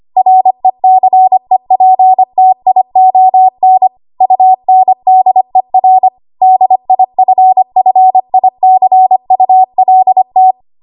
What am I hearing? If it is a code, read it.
Morse code, heard as a keyed tone, about 25 wpm, 750 Hz. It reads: RECEPTION UNDER DIFFICULT
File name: Archive / 25 WPM / 25WPM